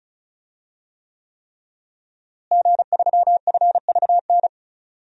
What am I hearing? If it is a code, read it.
Morse code, heard as a keyed tone, about 34 wpm, 700 Hz. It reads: G3FVN
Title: DXCC-100 Call Signs - 5 Characters - Repeated 3x - 35wpm